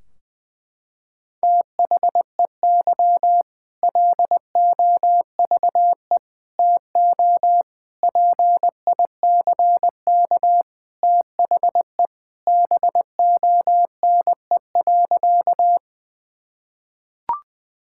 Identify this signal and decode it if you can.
Morse code, heard as a keyed tone, about 20 wpm, 700 Hz. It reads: THEY LOVE TO PICK THE BONE.
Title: Sentences from Top 1600 Words 20wpm